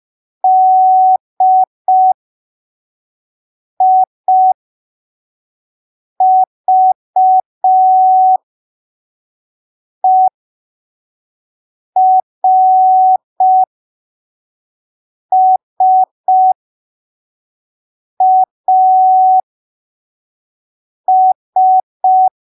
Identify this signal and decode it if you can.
Morse code, heard as a keyed tone, about 5 wpm, spaced as for 3 wpm, 750 Hz. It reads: DIVERSAS
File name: arquivos / texto / texto 06 05ppm